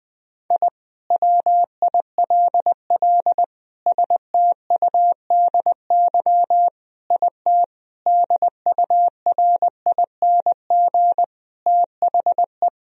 Morse code, heard as a keyed tone, about 19 wpm, 700 Hz. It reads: I WILL STUDY IT DURING THE
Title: Sentences from Top 1000 Words - Review Part 2 - 20wpm